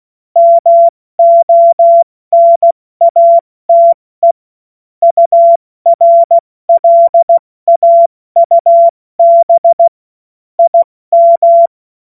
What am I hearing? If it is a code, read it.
Morse code, heard as a keyed tone, about 16 wpm, 680 Hz. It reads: MONATE URLAUB IM
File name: Morsekurs / Witze / Witz 09. August 2022 70 F80